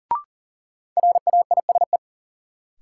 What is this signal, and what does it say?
Morse code, heard as a keyed tone, about 40 wpm, 700 Hz. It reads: RAISE
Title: Sentences from Top 1400 Words - New Word Drill - 40wpm